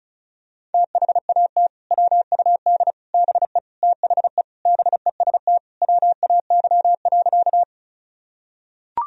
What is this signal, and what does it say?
Morse code, heard as a keyed tone, about 36 wpm, 700 Hz. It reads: THAT WUD BE THE BEST WAY.
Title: Sentences from Top 400 Words and Abbreviations 35wpm